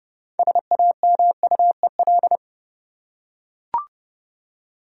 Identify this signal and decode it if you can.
Morse code, heard as a keyed tone, about 30 wpm, 700 Hz. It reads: SAMUEL
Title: USA Common Names - 30wpm